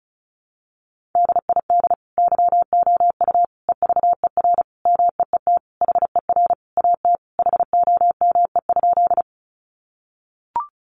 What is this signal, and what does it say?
Morse code, heard as a keyed tone, about 34 wpm, 700 Hz. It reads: DID YOU EVER MEET HER AT HOME?
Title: Sentences from Top 1100 Words 35wpm